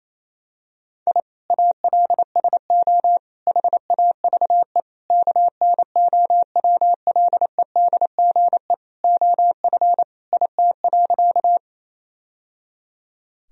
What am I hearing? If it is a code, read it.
Morse code, heard as a keyed tone, about 28 wpm, 700 Hz. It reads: I ALSO HAVE KNOWLEDGE OF IT.
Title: Sentences from Top 1000 Words - Review Part 3 - 28wpm